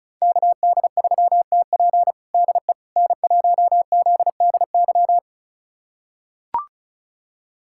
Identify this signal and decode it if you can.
Morse code, heard as a keyed tone, about 34 wpm, 700 Hz. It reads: KD3TP DE N1ZDY
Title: USA Callsign DE Callsign - 5 Characters - 35wpm